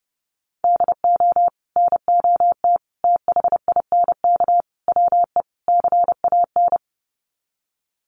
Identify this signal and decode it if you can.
Morse code, heard as a keyed tone, about 30 wpm, 700 Hz. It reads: DO NOT THINK WE CAN
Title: Sets of 5 Words Encore2 30wpm